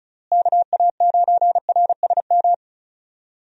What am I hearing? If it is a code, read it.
Morse code, heard as a keyed tone, about 34 wpm, 700 Hz. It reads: KA9RSM
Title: USA Call Signs - 6 Characters - Repeated 3x - 35wpm